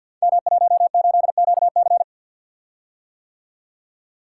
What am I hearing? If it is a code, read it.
Morse code, heard as a keyed tone, about 50 wpm, 700 Hz. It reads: M18QC
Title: Code Groups - 5 Letter:Numbers - 50wpm